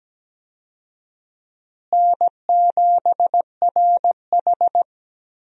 Morse code, heard as a keyed tone, about 17 wpm, 700 Hz. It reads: N7RH
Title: USA Call Signs - 4 Characters - Repeated 3x - 17wpm